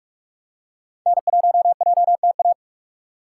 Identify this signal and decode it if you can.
Morse code, heard as a keyed tone, about 46 wpm, 700 Hz. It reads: N1JTA
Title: USA Call Signs - 5 Characters - 2x Character Spacing - 45wpm